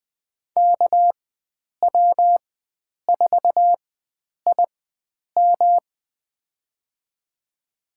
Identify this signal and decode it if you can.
Morse code, heard as a keyed tone, about 20 wpm, spaced as for 9 wpm, 700 Hz. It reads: KW4IM
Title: USA Call Signs - 5 Characters - 4x Character Spacing - 20wpm